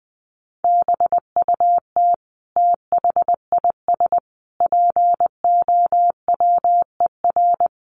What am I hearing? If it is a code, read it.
Morse code, heard as a keyed tone, about 19 wpm, 700 Hz. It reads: BUT THIS POWER
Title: Sentences from Top 1500 Words 20wpm